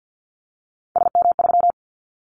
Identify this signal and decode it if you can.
Morse code, heard as a keyed tone, about 50 wpm, 700 Hz. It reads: SM3
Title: Callsign Prefixes for Top 25 DXCC - 50wpm